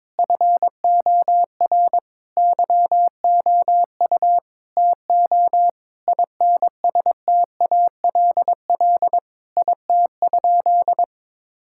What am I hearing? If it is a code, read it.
Morse code, heard as a keyed tone, about 22 wpm, 700 Hz. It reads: FOR YOU TO INSTALL IT?
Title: Sentences from Top 1600 Words 22wpm